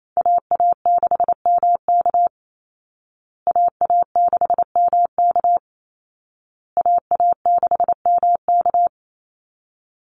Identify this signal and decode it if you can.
Morse code, heard as a keyed tone, about 28 wpm, 700 Hz. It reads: AA6MK AA6MK AA6MK
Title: Call Signs Repeated 2x - 28wpm